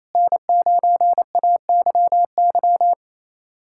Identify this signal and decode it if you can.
Morse code, heard as a keyed tone, about 28 wpm, 700 Hz. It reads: N9AYY
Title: USA Callsign DE Callsign - 5 Characters - 28wpm